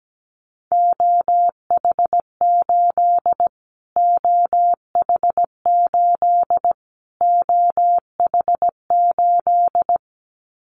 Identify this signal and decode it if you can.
Morse code, heard as a keyed tone, about 17 wpm, 700 Hz. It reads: OH8 OH8 OH8
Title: Callsign Prefixes for Top 25 DXCC - Repeated 3x - 17wpm